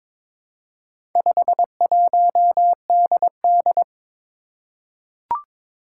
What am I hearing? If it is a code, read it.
Morse code, heard as a keyed tone, about 22 wpm, 700 Hz. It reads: 51DD
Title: Four Challenging Letter-Number Combinations 22wpm